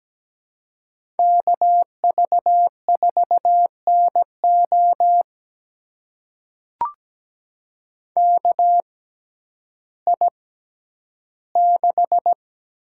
Morse code, heard as a keyed tone, about 17 wpm, spaced as for 10 wpm, 700 Hz. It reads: KV4NO KI6
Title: USA Call Signs - 5 Characters - 6x Character Spacing - 17wpm